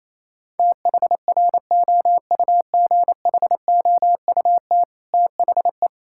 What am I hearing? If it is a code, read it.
Morse code, heard as a keyed tone, about 28 wpm, 700 Hz. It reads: THROUGHOUT THE
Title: Sentences from Top 1000 Words - Review Part 5 - 28wpm